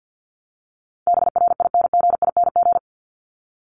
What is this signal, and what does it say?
Morse code, heard as a keyed tone, about 50 wpm, 700 Hz. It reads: BRINGING
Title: ING Suffix - Top 100 Words with Base Words 50wpm